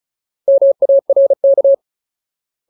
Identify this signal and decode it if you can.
Morse code, heard as a keyed tone, about 34 wpm, 550 Hz. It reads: MARK
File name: Ultimate Word List / 35WPM Ultimate